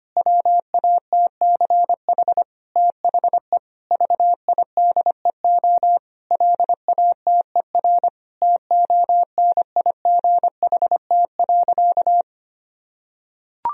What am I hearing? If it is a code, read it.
Morse code, heard as a keyed tone, about 25 wpm, 700 Hz. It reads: WATCH THE VIDEO LATER TONIGHT.
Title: Sentences from Top 1000 Words - Review Part 4 - 25wpm